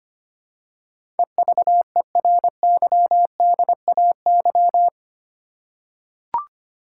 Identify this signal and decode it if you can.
Morse code, heard as a keyed tone, about 25 wpm, 700 Hz. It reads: EVERYDAY
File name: Eight Letter Words - 25wpm